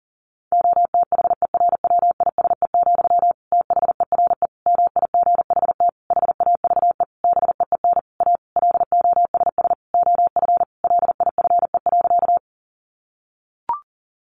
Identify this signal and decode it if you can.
Morse code, heard as a keyed tone, about 40 wpm, 700 Hz. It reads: OTHERWISE, THERE MIGHT HAVE BEEN A LOSS OF LIFE.
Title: Sentences from Top 1400 Words 40wpm